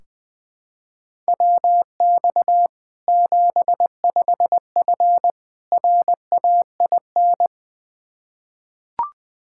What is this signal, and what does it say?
Morse code, heard as a keyed tone, about 21 wpm, 700 Hz. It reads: WX 75F RAIN
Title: Practice QSO - Exchange 2 - Info - 20wpm